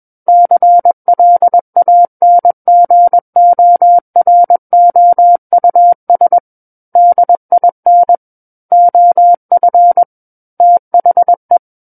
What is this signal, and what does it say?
Morse code, heard as a keyed tone, about 21 wpm, 700 Hz. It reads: CLANGOROUS DIN OF THE
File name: Book One / 12-Chapter 12 - 21 WPM